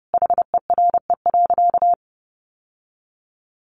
Morse code, heard as a keyed tone, about 30 wpm, 700 Hz. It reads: HERE.
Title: Sentences from Top 1200 Words 30wpm